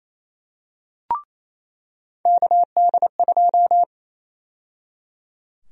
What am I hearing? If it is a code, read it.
Morse code, heard as a keyed tone, about 28 wpm, 700 Hz. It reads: KD2
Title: USA Call Sign Prefixes - 28wpm